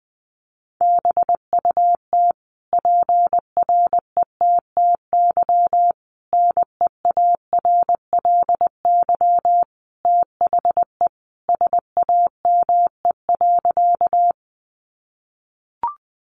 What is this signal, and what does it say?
Morse code, heard as a keyed tone, about 20 wpm, 700 Hz. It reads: BUT PRETTY NEARLY THE SAME.
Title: Sentences from Top 1300 Words 20wpm